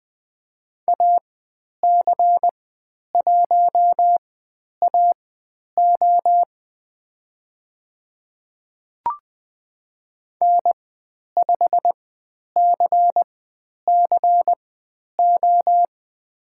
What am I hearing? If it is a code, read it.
Morse code, heard as a keyed tone, about 20 wpm, spaced as for 10 wpm, 700 Hz. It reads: AC1AO N5CCO
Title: 109 Lesson 18g - Letter m - Call Signs 20wpm